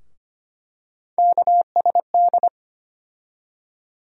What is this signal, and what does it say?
Morse code, heard as a keyed tone, about 25 wpm, 700 Hz. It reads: KSD
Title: USA Call Sign Postfixes - 3 Letters 25wpm